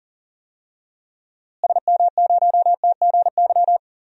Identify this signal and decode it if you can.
Morse code, heard as a keyed tone, about 38 wpm, 700 Hz. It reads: SM0TGY
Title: DXCC-100 Call Signs - 6 Characters - Repeated 2x - 40wpm